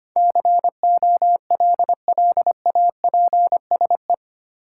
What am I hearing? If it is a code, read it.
Morse code, heard as a keyed tone, about 25 wpm, 700 Hz. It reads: COLLAPSE
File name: Eight Letter Words - 25wpm